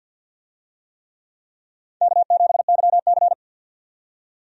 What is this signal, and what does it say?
Morse code, heard as a keyed tone, about 49 wpm, 700 Hz. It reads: K7YC
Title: USA Call Signs - 4 Characters - 50wpm